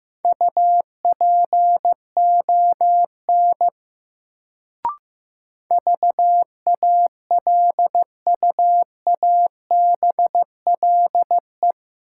Morse code, heard as a keyed tone, about 15 wpm, 700 Hz. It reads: UPON VALUABLE
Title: Sentences from Top 1100 Words - New Word Drill Alt - 15wpm